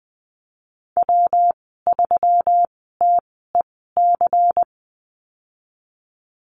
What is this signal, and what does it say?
Morse code, heard as a keyed tone, about 20 wpm, spaced as for 15 wpm, 700 Hz. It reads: W3TEC
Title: 200 Lesson 32g - Letter k - Call Signs 20wpm